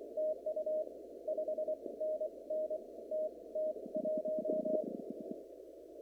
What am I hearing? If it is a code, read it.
Morse code, heard as a keyed tone, about 24 wpm, spaced as for 20 wpm, 620 Hz, with noise. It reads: TU 5NN TT1
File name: CW / kt5j wpx cw-op-k5pi